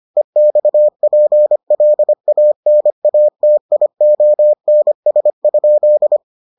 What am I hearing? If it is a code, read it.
Morse code, heard as a keyed tone, about 25 wpm, 600 Hz. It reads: EXPLANATIONS?